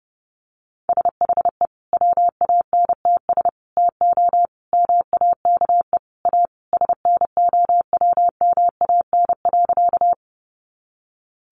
Morse code, heard as a keyed tone, about 30 wpm, 700 Hz. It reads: SHE WANTS TO MAKE A SNOWMAN.
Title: Easy Phrases 30wpm